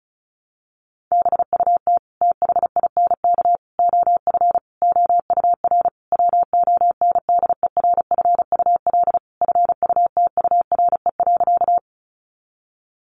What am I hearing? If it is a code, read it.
Morse code, heard as a keyed tone, about 34 wpm, 700 Hz. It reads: BUT THINK OF OUR WONDERFUL FUTURE.
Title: Sentences from Top 1100 Words 35wpm